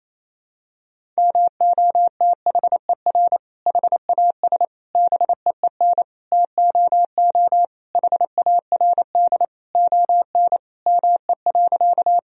Morse code, heard as a keyed tone, about 28 wpm, 700 Hz. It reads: MOTHER HAS BEEN TOO HARD ON ME.
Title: Sentences from Top 1000 Words - Review Part 1 - 28wpm